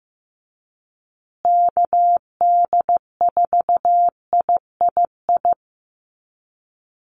Morse code, heard as a keyed tone, about 15 wpm, 700 Hz. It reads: KD4III
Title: USA Call Signs Repeated 2x 15wpm